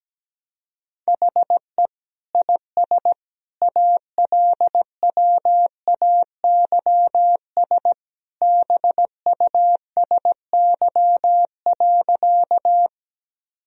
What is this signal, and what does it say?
Morse code, heard as a keyed tone, about 17 wpm, 700 Hz. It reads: HE IS ALWAYS BUSY.
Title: Sentences from Top 1000 Words - Review Part 3 - 17wpm